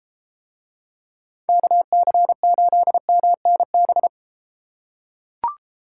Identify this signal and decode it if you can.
Morse code, heard as a keyed tone, about 32 wpm, 700 Hz. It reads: KC8MNB